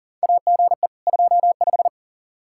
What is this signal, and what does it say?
Morse code, heard as a keyed tone, about 40 wpm, 700 Hz. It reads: AGE 25
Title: Practice QSO - Exchange 3 - More Info - 40wpm